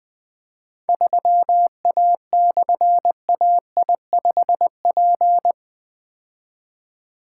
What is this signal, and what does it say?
Morse code, heard as a keyed tone, about 20 wpm, 700 Hz. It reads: 3A/AI5P
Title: Difficult Call Signs 20wpm